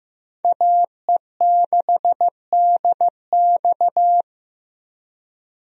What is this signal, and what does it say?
Morse code, heard as a keyed tone, about 15 wpm, 700 Hz. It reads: AE6DX
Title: USA Call Signs - 5 Characters - Repeated 3x - 15wpm